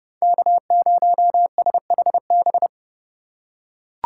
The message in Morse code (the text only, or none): K0SHB